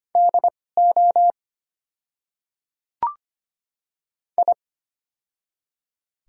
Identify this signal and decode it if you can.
Morse code, heard as a keyed tone, about 25 wpm, spaced as for 18 wpm, 700 Hz. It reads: DO I
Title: 2x Character Spacing - Top 500 Words - 25wpm